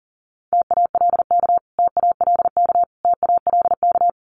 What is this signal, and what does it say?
Morse code, heard as a keyed tone, about 40 wpm, 700 Hz. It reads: TALK TALK TALK
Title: Top 500 Words Repeated 40wpm